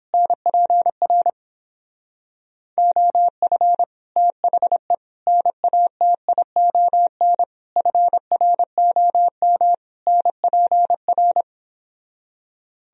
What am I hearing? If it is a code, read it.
Morse code, heard as a keyed tone, about 31 wpm, 700 Hz. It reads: NPR OF THE NATION FROM NPR